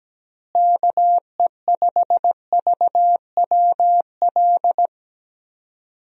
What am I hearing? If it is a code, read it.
Morse code, heard as a keyed tone, about 17 wpm, 700 Hz. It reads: KE5VWL
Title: USA Call Signs - 6 Characters - Repeated 2x - 17wpm